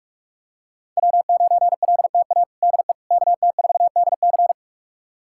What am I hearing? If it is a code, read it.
Morse code, heard as a keyed tone, about 46 wpm, 700 Hz. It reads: W9LTA DE KT4DC
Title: USA Callsign DE Callsign - 5 Characters - 45wpm